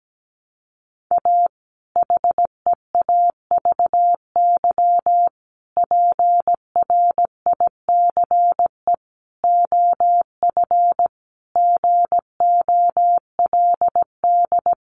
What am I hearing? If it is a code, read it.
Morse code, heard as a keyed tone, about 17 wpm, 700 Hz. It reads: A HEAVY PRICE OF GOLD
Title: Sentences from Top 1100 Words 17wpm